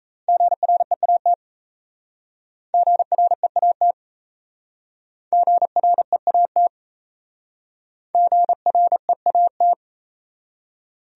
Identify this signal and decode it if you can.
Morse code, heard as a keyed tone, about 35 wpm, 700 Hz. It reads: GREAT GREAT GREAT GREAT
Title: Speed-Racing Top 300 Words 28wpm